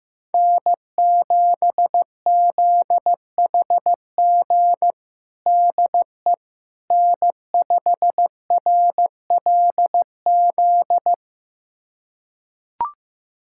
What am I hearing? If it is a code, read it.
Morse code, heard as a keyed tone, about 15 wpm, 700 Hz. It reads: N7ZHG DE N5RLZ
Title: USA Callsign DE Callsign - 5 Characters - 15wpm